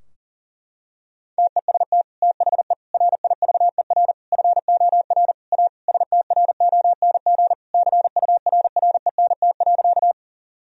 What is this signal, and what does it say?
Morse code, heard as a keyed tone, about 40 wpm, 700 Hz. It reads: TEST THE RIVER FOR A STRONG CURRENT.
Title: Sentences from Top 1000 Words - Review Part 4 - 40wpm